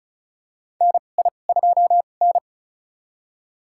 Morse code, heard as a keyed tone, about 36 wpm, spaced as for 25 wpm, 700 Hz. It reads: NI2N
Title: USA Call Signs - 2x Character Spacing - 35wpm